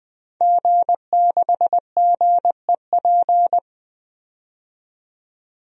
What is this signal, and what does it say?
Morse code, heard as a keyed tone, about 20 wpm, 700 Hz. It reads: G6GEP
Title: DXCC-100 Call Signs - 5 Characters - 20wpm